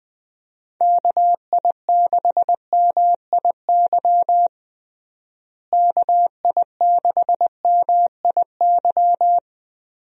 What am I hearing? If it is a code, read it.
Morse code, heard as a keyed tone, about 20 wpm, 700 Hz. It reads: KI6MIY KI6MIY
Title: USA Call Signs - 6 Characters - Repeated 2x - 20wpm